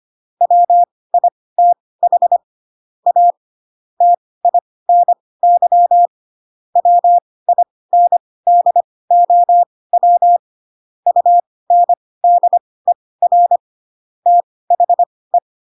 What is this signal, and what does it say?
Morse code, heard as a keyed tone, about 25 wpm, spaced as for 18 wpm, 700 Hz. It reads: WITH A TINY WINDOW UNDER THE